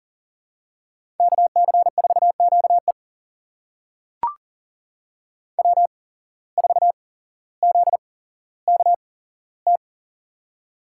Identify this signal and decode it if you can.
Morse code, heard as a keyed tone, about 42 wpm, spaced as for 17 wpm, 700 Hz. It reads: KC4QE W4ZKT
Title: USA Call Signs - 8x Character Spacing - 40wpm